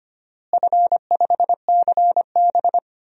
Morse code, heard as a keyed tone, about 25 wpm, 700 Hz. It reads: F5CB
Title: DXCC-100 Call Signs - 4 Characters - Repeated 2x - 25wpm